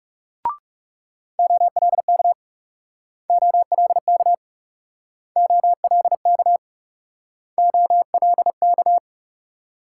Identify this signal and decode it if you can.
Morse code, heard as a keyed tone, about 41 wpm, 700 Hz. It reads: OLK OLK OLK OLK